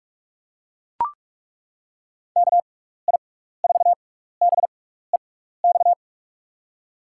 Morse code, heard as a keyed tone, about 42 wpm, spaced as for 15 wpm, 700 Hz. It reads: KI4BEX
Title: USA Call Signs - 6x Character Spacing - 45wpm